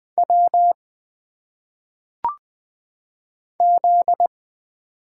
Morse code, heard as a keyed tone, about 20 wpm, 700 Hz. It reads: W Z
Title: Single Letters - Phonetic Pronunciation 20wpm